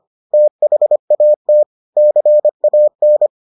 Morse code, heard as a keyed tone, about 25 wpm, 600 Hz. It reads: THAT CAN